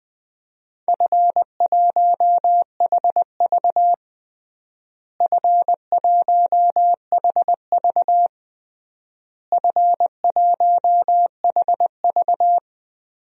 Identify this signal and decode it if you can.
Morse code, heard as a keyed tone, about 20 wpm, 700 Hz. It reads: F1HV F1HV F1HV
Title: DXCC-100 Call Signs - 4 Characters - Repeated 3x - 20wpm